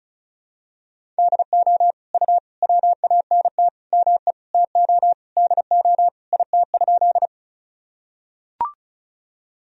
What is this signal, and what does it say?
Morse code, heard as a keyed tone, about 34 wpm, 690 Hz. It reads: DO U WANT ME TO DO IT?
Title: Sentences from Top 200 Words and Abbreviations 35wpm